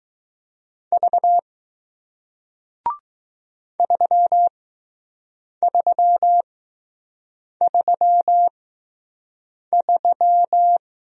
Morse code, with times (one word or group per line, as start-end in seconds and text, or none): 0.9-1.4 V
3.8-10.8 3333